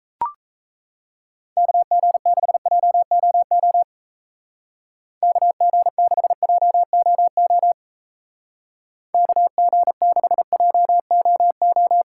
Morse code, heard as a keyed tone, about 35 wpm, 700 Hz. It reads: KG6JOO KG6JOO KG6JOO